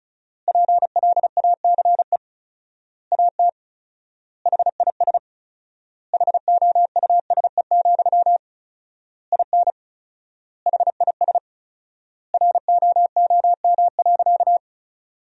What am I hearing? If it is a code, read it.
Morse code, heard as a keyed tone, about 36 wpm, 700 Hz. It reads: PLACE AT HIS HOUSE, IN HIS ROOM.
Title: 4x Word-Spacing - Sentences from Top 300 Words - 35wpm